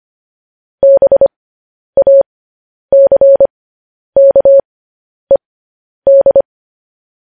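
Morse code, heard as a keyed tone, about 25 wpm, spaced as for 10 wpm, 550 Hz. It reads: BACKED
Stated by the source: Morse code news headlines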